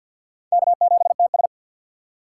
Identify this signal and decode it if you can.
Morse code, heard as a keyed tone, about 65 wpm, spaced as for 57 wpm, 700 Hz. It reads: K7TS
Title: Speed Racing - USA 1x2 Call Signs 50wpm